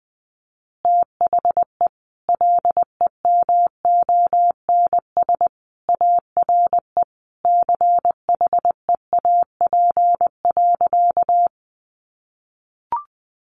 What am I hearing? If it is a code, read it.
Morse code, heard as a keyed tone, about 20 wpm, 700 Hz. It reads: THE LEMONS ARE CHEAP.
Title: Additional Easy Phrases 20wpm